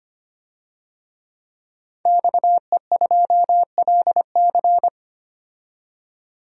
Morse code, heard as a keyed tone, about 25 wpm, 700 Hz. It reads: XE2LC
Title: DXCC-100 Call Signs - 5 Characters - 25wpm